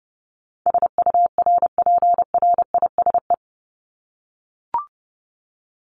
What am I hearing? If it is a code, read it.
Morse code, heard as a keyed tone, about 30 wpm, 700 Hz. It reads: SURPRISE
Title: Top 1751-2000 Words 30wpm